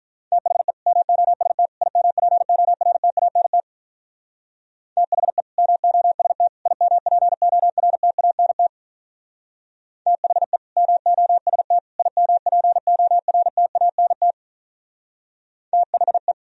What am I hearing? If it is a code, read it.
Morse code, heard as a keyed tone, about 42 wpm, 700 Hz. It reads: THE MOST IMPORTANT THE MOST IMPORTANT THE MOST IMPORTANT THE
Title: Speed-Racing - Sets of 3 Words 35wpm